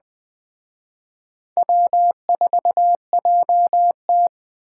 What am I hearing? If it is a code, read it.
Morse code, heard as a keyed tone, about 20 wpm, 700 Hz. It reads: W4JT
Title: Speed Racing - USA 1x2 Call Signs 17wpm